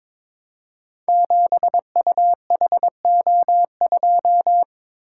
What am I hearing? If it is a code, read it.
Morse code, heard as a keyed tone, about 22 wpm, 700 Hz. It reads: 7UHO2
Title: Code Groups - 5 Letter:Numbers - 22wpm